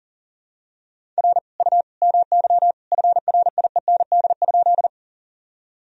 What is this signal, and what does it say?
Morse code, heard as a keyed tone, about 40 wpm, 700 Hz. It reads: R U MY FRIEND?
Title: Sentences from Top 300 Words and Abbreviations 40wpm